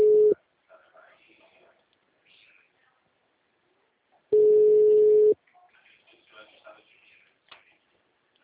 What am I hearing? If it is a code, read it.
Ringback tone (Europe, 425 Hz), with speech.